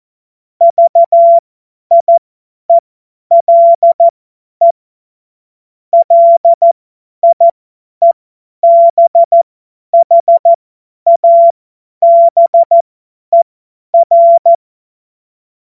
Morse code, heard as a keyed tone, about 14 wpm, spaced as for 10 wpm, 680 Hz. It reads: VIELE LIEBHABER
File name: Morsekurs / Witze / Witz 06. September 2022 50 F70